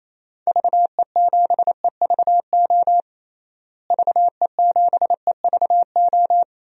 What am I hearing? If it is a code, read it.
Morse code, heard as a keyed tone, about 28 wpm, 700 Hz. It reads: VE7EVO VE7EVO
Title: DXCC-100 Call Signs - 6 Characters - Repeated 2x - 28wpm